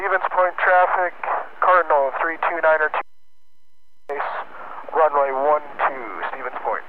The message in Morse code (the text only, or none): none